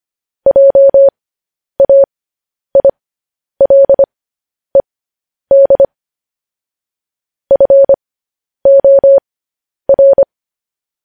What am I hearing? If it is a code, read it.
Morse code, heard as a keyed tone, about 25 wpm, spaced as for 10 wpm, 550 Hz. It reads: JAILED FOR
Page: News Headlines in Morse Code at 10 WPM